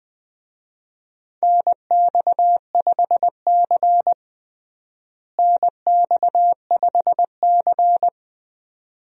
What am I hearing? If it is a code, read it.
Morse code, heard as a keyed tone, about 20 wpm, 700 Hz. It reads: NX5C NX5C
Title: USA Call Signs - 4 Characters - Repeated 2x - 20wpm